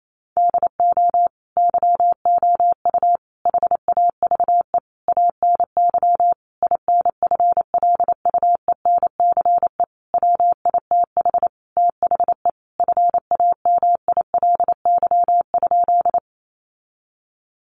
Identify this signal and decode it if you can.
Morse code, heard as a keyed tone, about 28 wpm, 700 Hz. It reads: DO YOU HAVE ANY INFLUENCE WITH THE FAMILY?
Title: Sentences from Top 1100 Words 28wpm